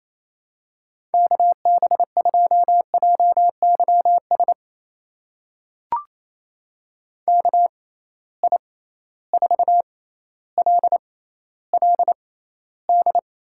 Morse code, heard as a keyed tone, about 28 wpm, spaced as for 14 wpm, 700 Hz. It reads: KB2JYS KI4LLD
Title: USA Call Signs - 6 Characters - 6x Character Spacing - 28wpm